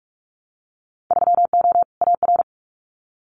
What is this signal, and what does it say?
Morse code, heard as a keyed tone, about 46 wpm, 700 Hz. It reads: 3O AR
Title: Winter Field Day Exchanges Encore 45wpm